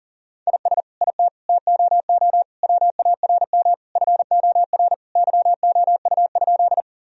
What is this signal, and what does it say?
Morse code, heard as a keyed tone, about 40 wpm, 700 Hz. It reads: IS IT TOO WARM FOR YOU?
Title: Sentences from Top 1000 Words - Review Part 2 - 40wpm